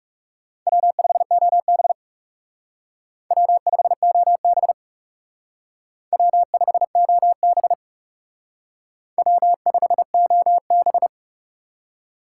Morse code, heard as a keyed tone, about 37 wpm, 700 Hz. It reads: W5OB W5OB W5OB W5OB
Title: Speed Racing - USA 1x2 Call Signs 30wpm